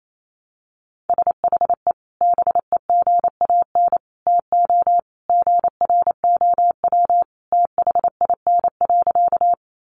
Morse code, heard as a keyed tone, about 28 wpm, 700 Hz. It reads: SHE BEGAN TO GROW THIN.
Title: Sentences from Top 1200 Words 28wpm